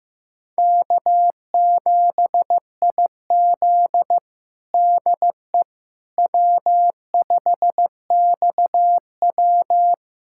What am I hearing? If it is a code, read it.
Morse code, heard as a keyed tone, about 16 wpm, 700 Hz. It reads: K7IZ DE W5XW
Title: USA Callsign DE Callsign - 4 Characters - 15wpm